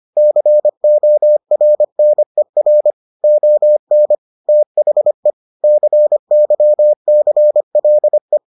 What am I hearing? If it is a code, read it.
Morse code, heard as a keyed tone, about 25 wpm, 600 Hz. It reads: CORNER ON THE CYCLE